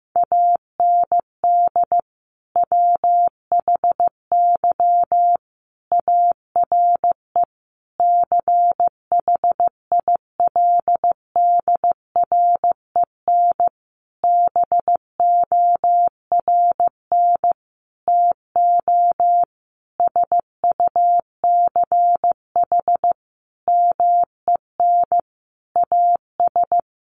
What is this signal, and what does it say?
Morse code, heard as a keyed tone, about 15 wpm, 700 Hz. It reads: AND WHY ARE CHILDREN BORN TO SUCH MEN AS
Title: Sentences from Top 1000 Words 15wpm